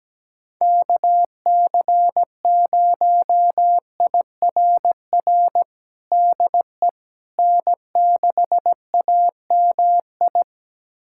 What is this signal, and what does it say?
Morse code, heard as a keyed tone, about 17 wpm, 700 Hz. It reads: KC0IRR DE N6AMI
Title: USA Callsign DE Callsign 17wpm